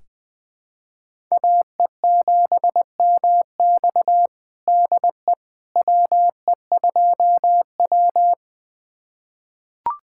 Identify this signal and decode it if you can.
Morse code, heard as a keyed tone, about 20 wpm, 700 Hz. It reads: AE7MX DE WE2W
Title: USA Callsign DE Callsign 20wpm